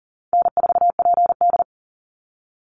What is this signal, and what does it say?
Morse code, heard as a keyed tone, about 38 wpm, 700 Hz. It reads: N4PD
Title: Call Signs Repeated 2x - 40wpm